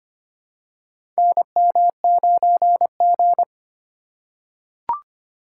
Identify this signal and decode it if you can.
Morse code, heard as a keyed tone, about 25 wpm, 700 Hz. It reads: NM9G
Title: USA Call Signs - 4 Characters - 3x Character Spacing - 25wpm